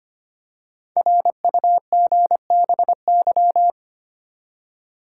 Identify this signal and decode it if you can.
Morse code, heard as a keyed tone, about 25 wpm, 700 Hz. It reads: RUGBY
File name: 5 Letter Words - First 3 Letters Given - 25wpm